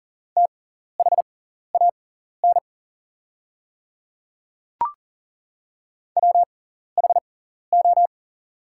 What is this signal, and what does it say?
Morse code, heard as a keyed tone, about 37 wpm, spaced as for 14 wpm, 700 Hz. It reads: THAN WHO
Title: 6x Character Spacing - Top 100 Words - 40wpm